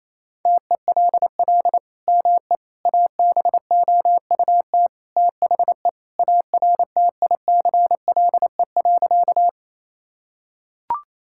Morse code, heard as a keyed tone, about 28 wpm, 700 Hz. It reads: TELL ME ABOUT THE ARTICLE.
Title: Sentences from Top 1000 Words - Review Part 5 - 28wpm